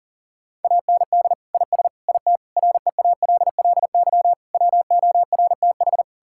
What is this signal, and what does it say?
Morse code, heard as a keyed tone, about 40 wpm, 700 Hz. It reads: AND IS IT REALLY WORTH
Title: Sentences from Top 1600 Words 40wpm